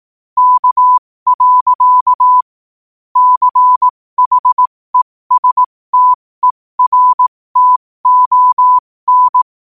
Morse code, heard as a keyed tone, about 18 wpm, spaced as for 15 wpm, 1000 Hz. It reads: K. CHESTERTON
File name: qotdfiles / 15wpm